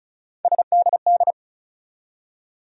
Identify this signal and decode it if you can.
Morse code, heard as a keyed tone, about 36 wpm, 700 Hz. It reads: SDD
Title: USA Call Sign Postfixes - 3 Letters 35wpm